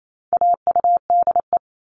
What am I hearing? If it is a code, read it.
Morse code, heard as a keyed tone, about 28 wpm, 700 Hz. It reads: AUDE
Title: All Cut Numbers - 4-Digit 28wpm